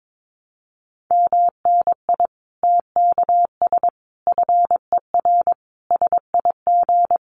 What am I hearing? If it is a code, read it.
Morse code, heard as a keyed tone, about 22 wpm, 700 Hz. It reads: MNI TKS FER SIG
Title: Common QSO Phrases 22wpm